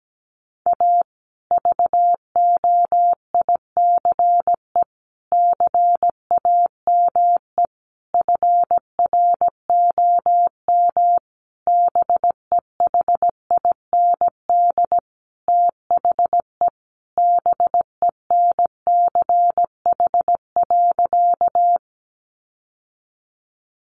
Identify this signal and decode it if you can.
Morse code, heard as a keyed tone, about 17 wpm, 700 Hz. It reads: A VOICE CAME FROM BEHIND THE BENCH.
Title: Sentences from Top 1400 Words 17wpm